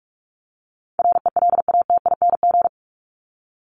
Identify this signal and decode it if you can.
Morse code, heard as a keyed tone, about 44 wpm, 700 Hz. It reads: RELATING
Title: ING Suffix - Top 100 Words with Base Words 45wpm